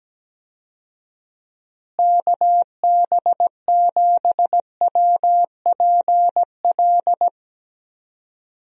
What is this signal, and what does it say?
Morse code, heard as a keyed tone, about 17 wpm, 700 Hz. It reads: KB7WPL
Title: USA Call Signs - 6 Characters - Repeated 2x - 17wpm